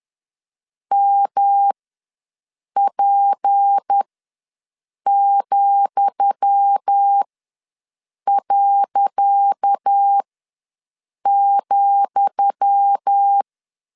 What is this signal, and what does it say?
Morse code, heard as a keyed tone, about 10 wpm, spaced as for 6 wpm, 800 Hz. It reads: MP,.,
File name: LEZIONI / 10 SIGNS / 10D COMA ws 10wpm 4314